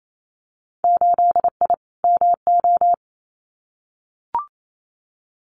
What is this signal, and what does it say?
Morse code, heard as a keyed tone, about 28 wpm, 700 Hz. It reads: 8I MO
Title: Winter Field Day Exchanges Encore 28wpm